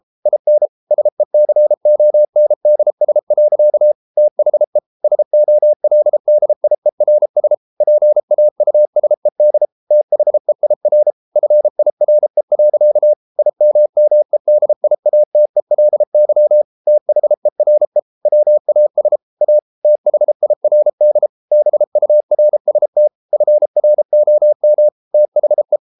Morse code, heard as a keyed tone, about 33 wpm, 600 Hz. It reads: IN SECONDS. THE SOLDIERS PAUSED THEIR FIRE. IMMEDIATELY THERE WAS A THIRD BURST FROM THE